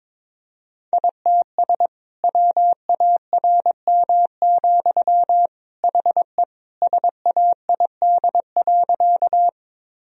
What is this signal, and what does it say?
Morse code, heard as a keyed tone, about 22 wpm, 700 Hz. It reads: ITS WARM, HE SAID.
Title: Sentences from Top 1000 Words - Review Part 2 - 22wpm